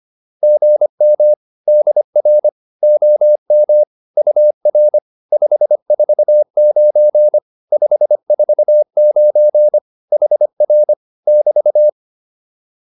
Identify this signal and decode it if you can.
Morse code, heard as a keyed tone, about 25 wpm, 600 Hz. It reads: GM DR OM UR 549 549 HR <BT>
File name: QSO204a 25